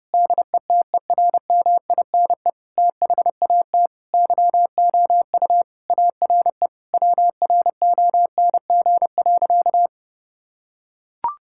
Morse code, heard as a keyed tone, about 30 wpm, 700 Hz. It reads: DETERMINE THAT YOU ARE WRONG.
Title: Sentences from Top 1000 Words - Review Part 5 - 30wpm